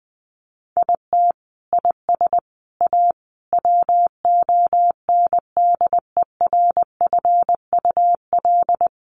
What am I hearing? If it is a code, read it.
Morse code, heard as a keyed tone, about 20 wpm, 700 Hz. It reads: IT IS A WONDERFUL
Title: Sentences from Top 1300 Words 20wpm